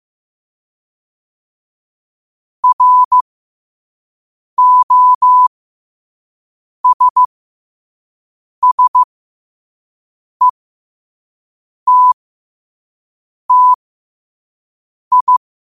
Morse code, heard as a keyed tone, about 15 wpm, spaced as for 5 wpm, 1000 Hz. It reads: ROSSETTI
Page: Quotes for Fri, 15 Aug 2025 in Morse Code at 5 words per minute.